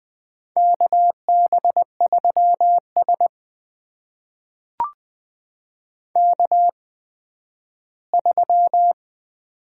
Morse code, heard as a keyed tone, about 20 wpm, spaced as for 12 wpm, 700 Hz. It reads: KB3S K3
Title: USA Call Signs - 4 Characters - 8x Character Spacing - 20wpm